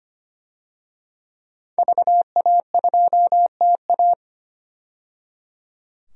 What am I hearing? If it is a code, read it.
Morse code, heard as a keyed tone, about 25 wpm, 700 Hz. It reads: VA2TA
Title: DXCC-100 Call Signs - 5 Characters - 25wpm